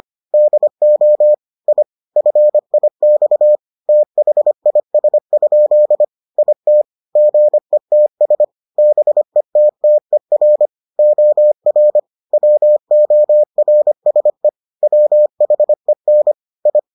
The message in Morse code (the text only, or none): DO I FIX THIS? IT GETS BETTER OR WORSE WHEN I